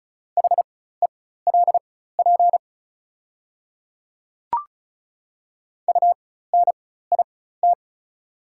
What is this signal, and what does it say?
Morse code, heard as a keyed tone, about 36 wpm, spaced as for 16 wpm, 690 Hz. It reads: HELP UNIT
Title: 4x Character Spacing - Top 500 Words - 35wpm